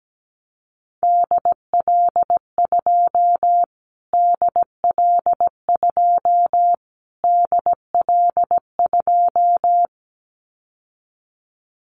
Morse code, heard as a keyed tone, about 17 wpm, 700 Hz. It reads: DL2 DL2 DL2
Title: Callsign Prefixes for Top 25 DXCC - Repeated 3x - 17wpm